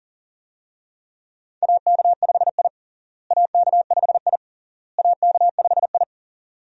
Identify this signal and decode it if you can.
Morse code, heard as a keyed tone, about 40 wpm, 700 Hz. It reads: AK5I AK5I AK5I
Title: USA Call Signs - 4 Characters - Repeated 3x - 40wpm